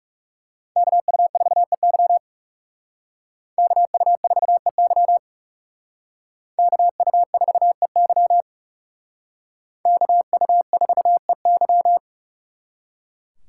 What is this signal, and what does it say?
Morse code, heard as a keyed tone, about 39 wpm, 700 Hz. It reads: KU4EY KU4EY KU4EY KU4EY